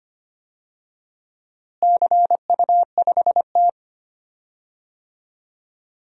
Morse code, heard as a keyed tone, about 25 wpm, 700 Hz. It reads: CU5T
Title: DXCC-100 Call Signs - 4 Characters - Repeated 2x - 25wpm